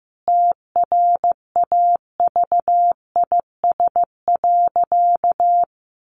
Morse code, heard as a keyed tone, about 15 wpm, 700 Hz. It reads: TRAVIS.
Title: A Sound Of Thunder 15wpm